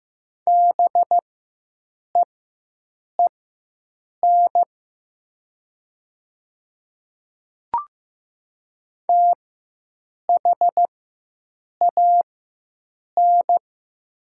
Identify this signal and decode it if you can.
Morse code, heard as a keyed tone, about 15 wpm, spaced as for 7 wpm, 700 Hz. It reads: BEEN THAN
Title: 4x Character Spacing - Top 300 Words - 15wpm